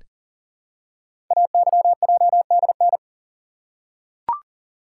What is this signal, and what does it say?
Morse code, heard as a keyed tone, about 39 wpm, 690 Hz. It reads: AYJDN
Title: Code Groups - 5 Letters - 40wpm